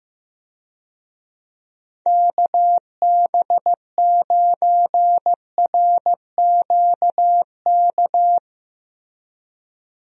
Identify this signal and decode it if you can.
Morse code, heard as a keyed tone, about 15 wpm, 700 Hz. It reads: KB9RQK
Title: USA Call Signs - 6 Characters - Repeated 2x - 15wpm